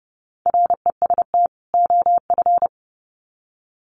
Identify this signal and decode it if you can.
Morse code, heard as a keyed tone, about 30 wpm, 700 Hz. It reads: REST OF
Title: Sets of 2 Words 30wpm